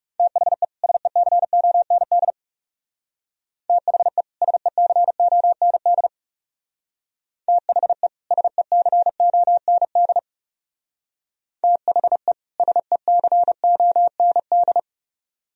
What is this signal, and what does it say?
Morse code, heard as a keyed tone, about 37 wpm, 700 Hz. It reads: THE SECOND THE SECOND THE SECOND THE SECOND
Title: Speed-Racing - Sets of 2 Words 30wpm